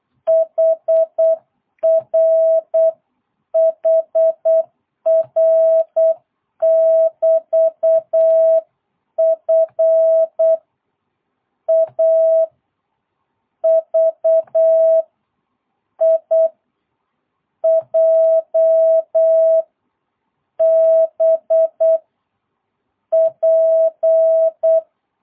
Morse code, heard as a keyed tone, about 8 wpm, 650 Hz. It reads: HR HR <BT> F A V I J B P